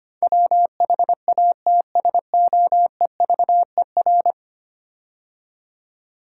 Morse code, heard as a keyed tone, about 25 wpm, 700 Hz. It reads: WHATSOEVER
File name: 10 Letter Words - First 3 Letters Given - 25wpm